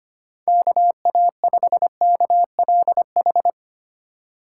Morse code, heard as a keyed tone, about 25 wpm, 700 Hz. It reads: KA5KLH
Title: USA Call Signs - 6 Characters - 4x Character Spacing - 25wpm